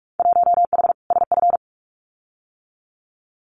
Morse code, heard as a keyed tone, about 46 wpm, 700 Hz. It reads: 1H SF
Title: Winter Field Day Exchanges 45wpm